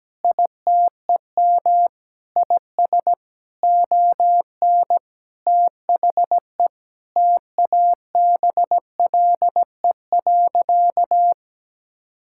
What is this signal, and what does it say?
Morse code, heard as a keyed tone, about 17 wpm, 700 Hz. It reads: ITEM IS ON THE TABLE.
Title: Sentences from Top 1000 Words - Review Part 6 - 17wpm